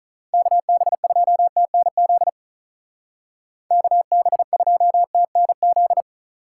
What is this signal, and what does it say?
Morse code, heard as a keyed tone, about 43 wpm, 700 Hz. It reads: KB2TNZ KB2TNZ
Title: Speed Racing - USA 2x3 Call Signs 35wpm